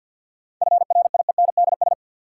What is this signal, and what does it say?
Morse code, heard as a keyed tone, about 46 wpm, 700 Hz. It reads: FRIENDS
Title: Sentences from Top 1000 Words - Review Part 5 - New Word Drill - 50wpm